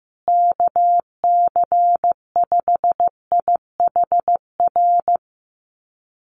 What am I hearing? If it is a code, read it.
Morse code, heard as a keyed tone, about 15 wpm, 700 Hz. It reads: KC5IHR
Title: USA Call Signs Repeated 3x 15wpm